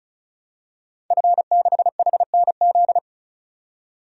Morse code, heard as a keyed tone, about 36 wpm, 700 Hz. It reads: F6HNZ
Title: DXCC-100 Call Signs - 5 Characters - 35wpm